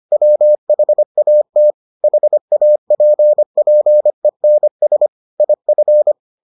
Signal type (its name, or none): Morse code